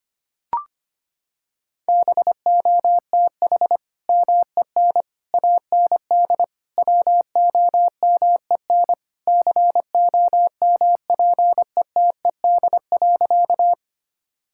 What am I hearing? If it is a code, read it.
Morse code, heard as a keyed tone, about 25 wpm, 700 Hz. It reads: BOTH MEN AND WOMEN COMPETED.
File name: practice / 25wpm / Sentences with 3-Word Phrases - 25wpm